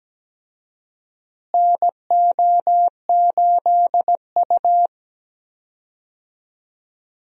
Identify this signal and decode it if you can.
Morse code, heard as a keyed tone, about 17 wpm, 700 Hz. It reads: NO8U
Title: USA Call Signs - 4 Characters - Repeated 3x - 17wpm